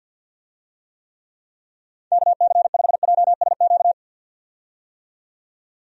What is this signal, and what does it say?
Morse code, heard as a keyed tone, about 53 wpm, 700 Hz. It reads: KC5JSQ
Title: USA Call Signs - 6 Characters - 50wpm